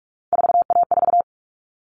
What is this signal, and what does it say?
Morse code, heard as a keyed tone, about 44 wpm, 700 Hz. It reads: 4A4
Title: Common Cut Numbers - 3-Digit 45wpm